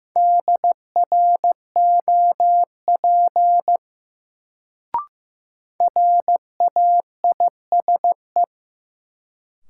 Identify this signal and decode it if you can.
Morse code, heard as a keyed tone, about 15 wpm, 700 Hz. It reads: DROP RAISE
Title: Sentences from Top 1400 Words - New Word Drill - 15wpm